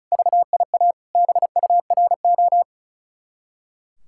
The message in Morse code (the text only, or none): VIA BURO